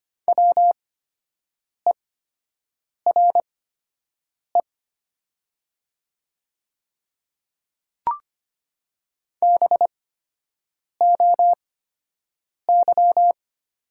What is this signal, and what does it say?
Morse code, heard as a keyed tone, about 25 wpm, spaced as for 7 wpm, 700 Hz. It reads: WERE BOY
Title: 8x Character Spacing - Top 200 Words - 25wpm